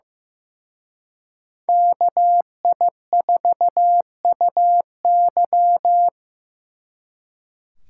Morse code, heard as a keyed tone, about 18 wpm, 700 Hz. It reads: KI4UY
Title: Speed Racing - USA 2x2 Call Signs 15wpm